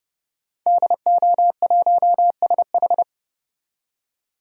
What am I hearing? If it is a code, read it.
Morse code, heard as a keyed tone, about 32 wpm, 700 Hz. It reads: DO1SH
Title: DXCC-100 Call Signs - 5 Characters - 30wpm